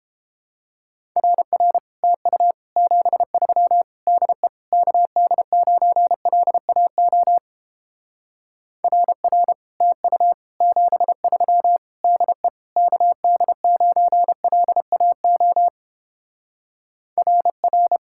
Morse code, heard as a keyed tone, about 29 wpm, 700 Hz. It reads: RR TU 73 DE KD9LAO RR TU 73 DE KD9LAO RR